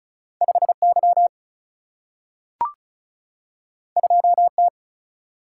Morse code, heard as a keyed tone, about 36 wpm, 700 Hz. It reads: 5Y 2T
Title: Single-Letter Single-Number 35wpm